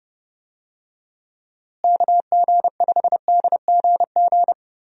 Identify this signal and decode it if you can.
Morse code, heard as a keyed tone, about 31 wpm, 700 Hz. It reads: KG5DGG
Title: USA Call Signs - 6 Characters - Repeated 3x - 30wpm